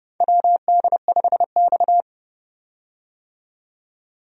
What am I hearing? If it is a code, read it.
Morse code, heard as a keyed tone, about 30 wpm, 700 Hz. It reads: WD5X
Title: USA Call Signs - 4 Characters - 30wpm